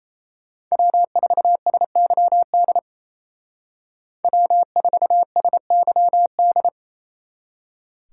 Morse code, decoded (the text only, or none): W4SYD W4SYD